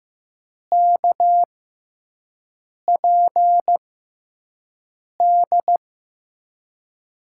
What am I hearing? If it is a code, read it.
Morse code, heard as a keyed tone, about 15 wpm, spaced as for 5 wpm, 700 Hz. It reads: KPD
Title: USA Call Signs - 6 Characters - 6x Character Spacing - 15wpm